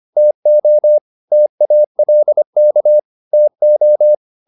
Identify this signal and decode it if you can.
Morse code, heard as a keyed tone, about 25 wpm, 600 Hz. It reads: TO TALK TO
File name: the old man and the sea part 1 25wpm